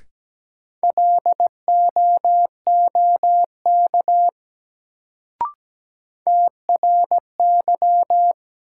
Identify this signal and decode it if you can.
Morse code, heard as a keyed tone, about 17 wpm, 700 Hz. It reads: LOOK TRY
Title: Sets of 2 Words - New Word Drill - 17wpm